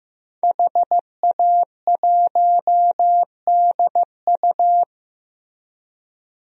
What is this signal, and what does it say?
Morse code, heard as a keyed tone, about 15 wpm, 700 Hz. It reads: HA1DU
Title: DXCC-100 Call Signs - 5 Characters - Repeated 2x - 15wpm